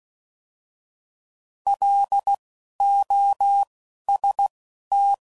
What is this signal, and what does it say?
Morse code, heard as a keyed tone, about 16 wpm, spaced as for 11 wpm, 800 Hz. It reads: LOST